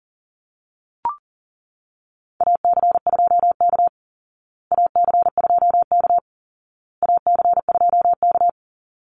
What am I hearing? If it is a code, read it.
Morse code, heard as a keyed tone, about 40 wpm, 700 Hz. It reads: AC2K AC2K AC2K
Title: Call Signs Repeated 2x - 40wpm